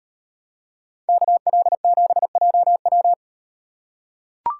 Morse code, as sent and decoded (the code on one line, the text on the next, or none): -.- .-.. --... .--- .--
KL7JW